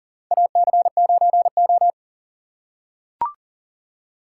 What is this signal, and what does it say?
Morse code, heard as a keyed tone, about 40 wpm, 700 Hz. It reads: AC9O
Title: USA Call Signs - 4 Characters - 2x Character Spacing - 40wpm